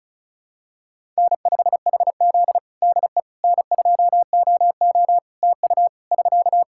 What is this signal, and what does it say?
Morse code, heard as a keyed tone, about 34 wpm, 700 Hz. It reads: N5HZ DE N2OO TU <SK>
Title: Practice QSO - Exchange 4 - Ending - 35wpm